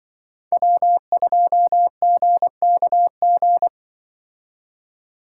Morse code, heard as a keyed tone, about 25 wpm, 700 Hz. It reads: W2GKG
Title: Speed Racing - USA 1x3 Call Signs 20wpm